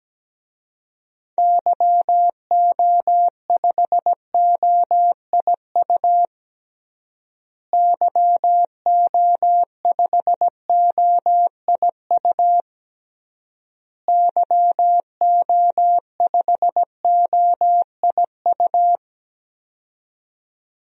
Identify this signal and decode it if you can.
Morse code, heard as a keyed tone, about 17 wpm, 700 Hz. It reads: YO5OIU YO5OIU YO5OIU
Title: DXCC-100 Call Signs - 6 Characters - Repeated 3x - 17wpm